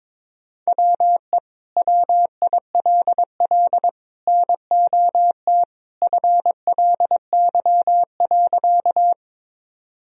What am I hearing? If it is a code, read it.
Morse code, heard as a keyed tone, about 22 wpm, 700 Hz. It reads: WE WILL NOT FLY.
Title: Sentences from Top 1000 Words - Review Part 2 - 22wpm